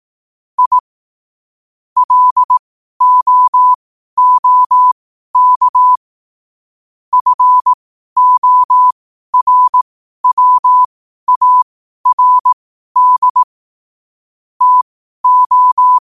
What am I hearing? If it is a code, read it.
Morse code, heard as a keyed tone, about 18 wpm, spaced as for 12 wpm, 1000 Hz. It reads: I LOOK FORWARD TO
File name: qotdfiles / 12wpm